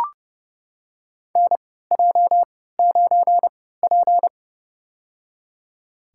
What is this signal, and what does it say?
Morse code, heard as a keyed tone, about 30 wpm, spaced as for 17 wpm, 700 Hz. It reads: NJ9P
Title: USA Call Signs - 4 Characters - 3x Character Spacing - 30wpm